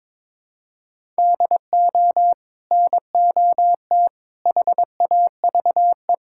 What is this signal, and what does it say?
Morse code, heard as a keyed tone, about 22 wpm, 700 Hz. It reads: DO NOT HAVE
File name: practice / 22wpm / Build A Phrase - 5-Word Phrases - 22wpm